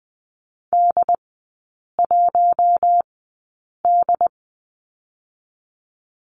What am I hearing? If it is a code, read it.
Morse code, heard as a keyed tone, about 20 wpm, spaced as for 8 wpm, 700 Hz. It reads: D1D
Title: 067 Lesson 12e - Letter d - Three Characters 20wpm